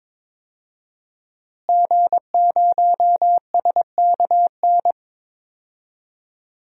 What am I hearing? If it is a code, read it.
Morse code, heard as a keyed tone, about 22 wpm, 700 Hz. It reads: G0SKN
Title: DXCC-100 Call Signs - 5 Characters - Repeated 2x - 22wpm